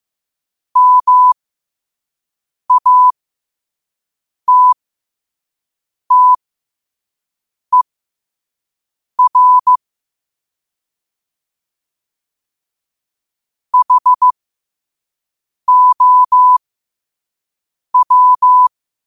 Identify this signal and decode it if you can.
Morse code, heard as a keyed tone, about 15 wpm, spaced as for 5 wpm, 1000 Hz. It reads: MATTER HOW
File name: qotdfiles / 5wpm